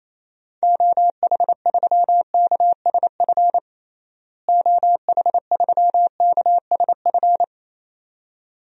Morse code, heard as a keyed tone, about 28 wpm, 700 Hz. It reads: OH3KSF OH3KSF
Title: DXCC-100 Call Signs - 6 Characters - Repeated 2x - 28wpm